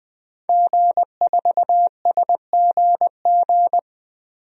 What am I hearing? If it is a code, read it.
Morse code, heard as a keyed tone, about 20 wpm, 700 Hz. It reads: G4SGG
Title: DXCC-100 Call Signs - 5 Characters - 20wpm